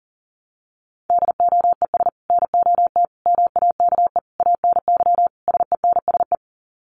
Morse code, heard as a keyed tone, about 40 wpm, 700 Hz. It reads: DOES NOT MAKE ANY SENSE
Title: Sets of 5 Words Encore2 40wpm